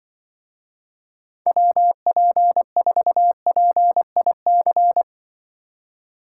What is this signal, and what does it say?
Morse code, heard as a keyed tone, about 24 wpm, 700 Hz. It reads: WP4PIC
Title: Speed Racing - USA 2x3 Call Signs 20wpm